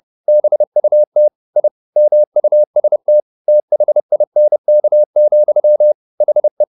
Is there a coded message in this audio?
Yes: Morse code from 0.3 to 6.6 s, heard as a keyed tone, about 30 wpm, 600 Hz. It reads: BUT I MUST THINK, HE